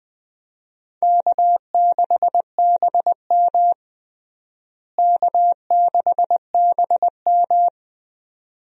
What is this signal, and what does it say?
Morse code, heard as a keyed tone, about 20 wpm, 700 Hz. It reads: K6BM K6BM
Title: USA Call Signs - 4 Characters - Repeated 2x - 20wpm